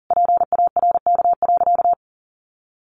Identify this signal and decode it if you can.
Morse code, heard as a keyed tone, about 40 wpm, 700 Hz. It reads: PARK.
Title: Sentences from Top 1400 Words 40wpm